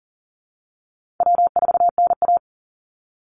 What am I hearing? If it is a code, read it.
Morse code, heard as a keyed tone, about 40 wpm, 700 Hz. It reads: W4NA
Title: Easy Call Signs - 40wpm